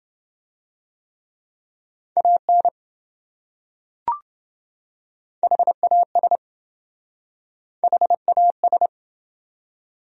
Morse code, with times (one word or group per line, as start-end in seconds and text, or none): 2.2-2.7 AN
5.4-6.4 HAS
7.8-8.9 HAS